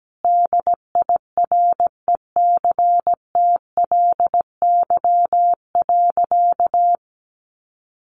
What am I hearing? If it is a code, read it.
Morse code, heard as a keyed tone, about 17 wpm, 700 Hz. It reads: DIRECTLY.
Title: Sentences from Top 800 Words 17wpm